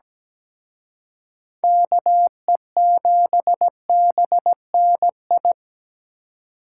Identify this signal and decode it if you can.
Morse code, heard as a keyed tone, about 20 wpm, 700 Hz. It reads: KE7BNI
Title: Speed Racing - USA 2x3 Call Signs 17wpm